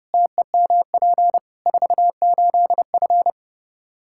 Morse code, heard as a keyed tone, about 32 wpm, 700 Hz. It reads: TEMP 48F
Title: Practice QSO - Exchange 2 - Info - 30wpm